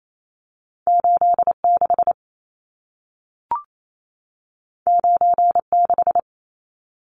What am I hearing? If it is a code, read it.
Morse code, heard as a keyed tone, about 28 wpm, 700 Hz. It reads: 86 96
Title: Numbers - 2-Digit 28wpm